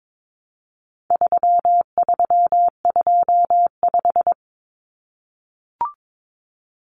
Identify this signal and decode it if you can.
Morse code, heard as a keyed tone, about 22 wpm, 700 Hz. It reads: 3325
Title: Common Cut Numbers - 4-Digit 22wpm